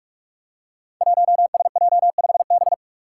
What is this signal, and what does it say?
Morse code, heard as a keyed tone, about 44 wpm, 700 Hz. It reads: 1SJ5B
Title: Five Challenging Letter-Number Combinations 45wpm